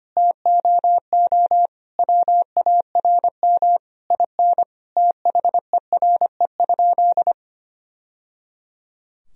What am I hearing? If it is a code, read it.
Morse code, heard as a keyed tone, about 25 wpm, 700 Hz. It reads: TOO WARM IN THERE?
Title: Sentences from Top 1000 Words - Review Part 4 - 25wpm